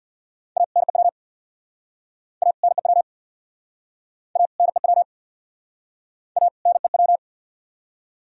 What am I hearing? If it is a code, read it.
Morse code, heard as a keyed tone, about 67 wpm, 700 Hz. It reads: A NEW A NEW A NEW A NEW